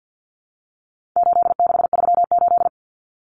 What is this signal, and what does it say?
Morse code, heard as a keyed tone, about 50 wpm, 700 Hz. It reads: 8638
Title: Common Cut Numbers - 4-Digit 50wpm